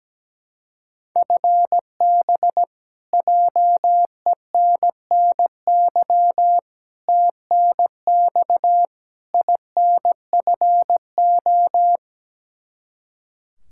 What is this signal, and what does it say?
Morse code, heard as a keyed tone, about 17 wpm, 700 Hz. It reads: FB JENNY TNX INFO
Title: Practice QSO - Exchange 3 - More Info - 17wpm